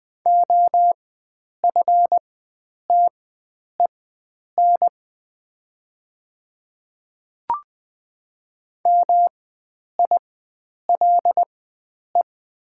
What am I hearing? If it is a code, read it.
Morse code, heard as a keyed tone, about 20 wpm, spaced as for 9 wpm, 700 Hz. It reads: OFTEN MILE
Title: 4x Character Spacing - Top 300 Words - 20wpm